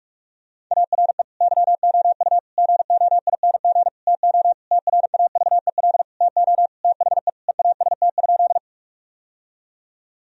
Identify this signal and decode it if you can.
Morse code, heard as a keyed tone, about 46 wpm, 700 Hz. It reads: ARE YOU GOING TO TRAVEL TO THE EAST?
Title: Sentences from Top 1000 Words - Review Part 3 - 45wpm